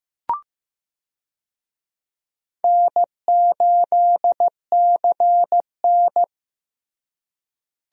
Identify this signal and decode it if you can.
Morse code, heard as a keyed tone, about 15 wpm, 700 Hz. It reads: N8CN
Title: USA Call Signs - 4 Characters - Repeated 2x - 15wpm